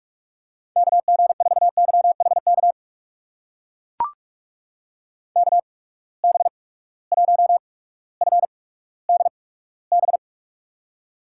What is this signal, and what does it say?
Morse code, heard as a keyed tone, about 43 wpm, spaced as for 19 wpm, 700 Hz. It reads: KG4YHK KB1FDB
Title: USA Call Signs - 6 Characters - 8x Character Spacing - 45wpm